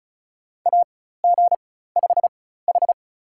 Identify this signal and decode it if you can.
Morse code, heard as a keyed tone, about 36 wpm, spaced as for 16 wpm, 700 Hz. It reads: AG5H
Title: USA Call Signs - 4 Characters - 4x Character Spacing - 35wpm